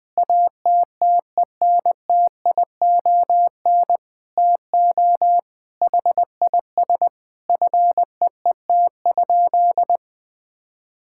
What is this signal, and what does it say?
Morse code, heard as a keyed tone, about 20 wpm, 700 Hz. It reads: ATTENTION TO HIS FEET?
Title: Sentences from Top 1000 Words - Review Part 6 - 20wpm